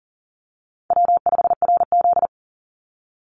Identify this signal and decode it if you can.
Morse code, heard as a keyed tone, about 40 wpm, 700 Hz. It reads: W5RZ
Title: Call Signs Repeated 3x - 40wpm